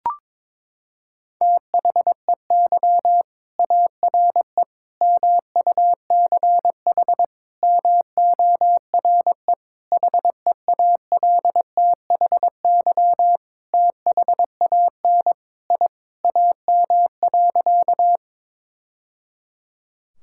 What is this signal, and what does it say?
Morse code, heard as a keyed tone, about 22 wpm, 700 Hz. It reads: THEY ARE MUCH MORE HEALTHY THAN I AM.
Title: Sentences from Top 1000 Words - Review Part 5 - 22wpm